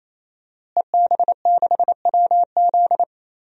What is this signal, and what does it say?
Morse code, heard as a keyed tone, about 28 wpm, 690 Hz. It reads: EB6WZ
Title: DXCC-100 Call Signs - 5 Characters - Repeated 3x - 28wpm